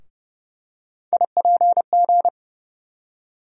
Morse code, heard as a keyed tone, about 30 wpm, 700 Hz. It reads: IPG
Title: USA Call Sign Postfixes - 3 Letters 30wpm